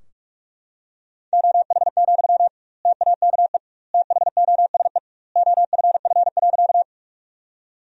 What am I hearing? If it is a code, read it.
Morse code, heard as a keyed tone, about 46 wpm, 700 Hz. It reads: OH, TAKE THOSE OFF.
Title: 2x Word-Spacing - Sentences from Top 500 Words - 45wpm